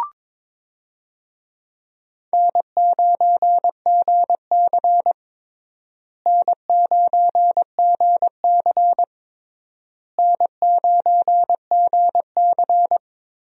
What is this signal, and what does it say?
Morse code, heard as a keyed tone, about 21 wpm, 700 Hz. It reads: N9GC N9GC N9GC
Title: USA Call Signs - 4 Characters - Repeated 3x - 22wpm